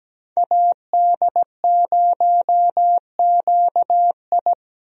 Morse code, heard as a keyed tone, about 17 wpm, 700 Hz. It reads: AD0QI
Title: USA Call Signs - 5 Characters - 17wpm